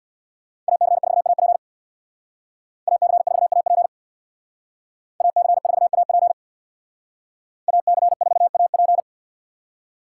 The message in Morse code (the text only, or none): AC4AP AC4AP AC4AP AC4AP